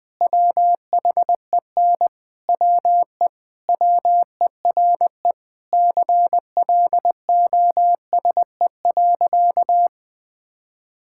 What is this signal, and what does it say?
Morse code, heard as a keyed tone, about 20 wpm, 700 Hz. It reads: WHEN WE WERE CLOSE.
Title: Sentences with 5-Word Phrases - 20wpm